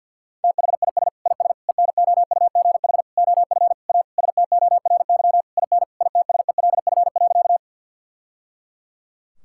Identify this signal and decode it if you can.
Morse code, heard as a keyed tone, about 50 wpm, 700 Hz. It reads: THIS IS ENOUGH OF A STORY IN ITSELF.
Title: Sentences from Top 1000 Words - Review Part 5 - 50wpm